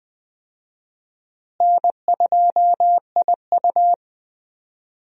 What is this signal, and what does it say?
Morse code, heard as a keyed tone, about 20 wpm, 700 Hz. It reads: N2IU
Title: USA Call Signs - 4 Characters - Repeated 2x - 20wpm